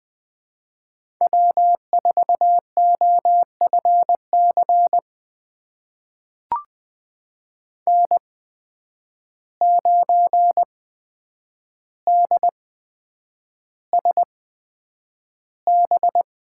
Morse code, heard as a keyed tone, about 20 wpm, spaced as for 9 wpm, 700 Hz. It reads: W4OFC N9DSB
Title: USA Call Signs - 5 Characters - 8x Character Spacing - 20wpm